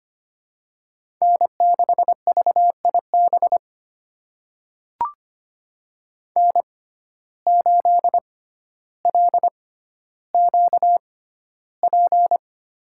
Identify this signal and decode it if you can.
Morse code, heard as a keyed tone, about 25 wpm, spaced as for 13 wpm, 700 Hz. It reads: N6VIB N8LQP
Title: USA Call Signs - 5 Characters - 6x Character Spacing - 25wpm